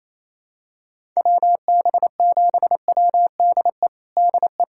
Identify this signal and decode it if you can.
Morse code, heard as a keyed tone, about 28 wpm, 700 Hz. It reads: WB7WDE DE
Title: USA Callsign DE Callsign - 6 Characters - 28wpm